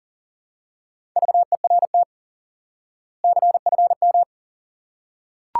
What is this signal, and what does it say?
Morse code, heard as a keyed tone, about 40 wpm, 700 Hz. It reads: VERT CFM
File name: IQE Course 3 - Lesson 89 - YAGI - Two Elements - 40wpm